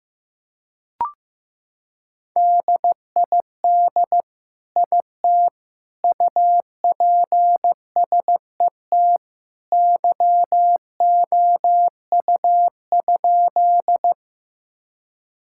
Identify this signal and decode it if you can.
Morse code, heard as a keyed tone, about 15 wpm, 690 Hz. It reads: DID IT UPSET YOU?
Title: Sentences from Top 1000 Words - Review Part 5 - 15wpm